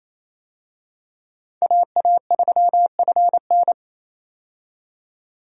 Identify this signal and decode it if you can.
Morse code, heard as a keyed tone, about 28 wpm, 700 Hz. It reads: AA3FN
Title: USA Call Signs - 5 Characters - 28wpm